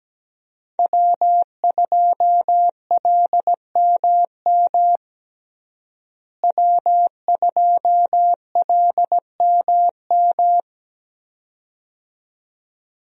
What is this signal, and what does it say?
Morse code, heard as a keyed tone, about 17 wpm, 700 Hz. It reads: W2LMM W2LMM
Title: USA Call Signs - 5 Characters - Repeated 2x - 17wpm